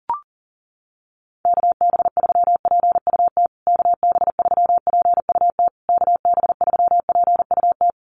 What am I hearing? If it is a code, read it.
Morse code, heard as a keyed tone, about 40 wpm, 700 Hz. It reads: KB3PUT KB3PUT KB3PUT
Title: USA Call Signs Repeated 3x 40wpm